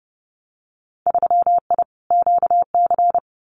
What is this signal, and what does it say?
Morse code, heard as a keyed tone, about 30 wpm, 700 Hz. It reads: 3I QC
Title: Winter Field Day Exchanges Encore 30wpm